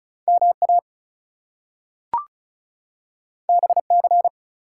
Morse code, heard as a keyed tone, about 36 wpm, 700 Hz. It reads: MA BC
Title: US States and Canadian Provinces 35wpm